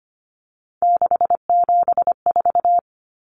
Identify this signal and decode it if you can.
Morse code, heard as a keyed tone, about 25 wpm, 700 Hz. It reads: 674
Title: Common Cut Numbers - 3-Digit 25wpm